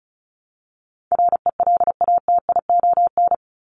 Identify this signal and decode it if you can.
Morse code, heard as a keyed tone, about 34 wpm, 700 Hz. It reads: RELATION
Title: Top 1751-2000 Words 35wpm